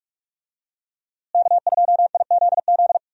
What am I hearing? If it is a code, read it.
Morse code, heard as a keyed tone, about 46 wpm, 700 Hz. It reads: K2IZZ
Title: USA Call Signs - 5 Characters - Repeated 2x - 45wpm